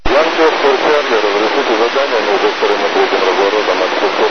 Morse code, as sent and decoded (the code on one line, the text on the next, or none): none